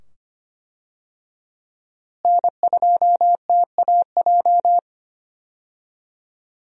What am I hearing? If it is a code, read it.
Morse code, heard as a keyed tone, about 25 wpm, 700 Hz. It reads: N2TAJ
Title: USA Call Signs - 5 Characters - Repeated 2x - 25wpm